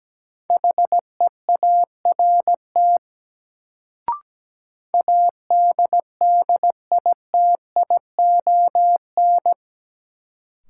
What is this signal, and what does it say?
Morse code, heard as a keyed tone, about 17 wpm, 700 Hz. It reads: HEART ADDITION
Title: Sets of 3 Words - New Word Drill - 17wpm